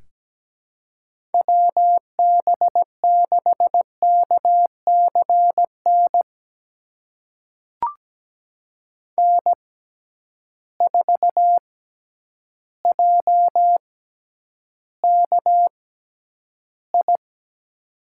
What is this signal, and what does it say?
Morse code, heard as a keyed tone, about 17 wpm, spaced as for 9 wpm, 700 Hz. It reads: WB6KCN N4JKI
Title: USA Call Signs - 6x Character Spacing - 17wpm